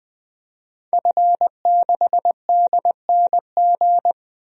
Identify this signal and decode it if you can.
Morse code, heard as a keyed tone, about 21 wpm, 700 Hz. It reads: F6DNG
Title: DXCC-100 Call Signs - 5 Characters - Repeated 3x - 20wpm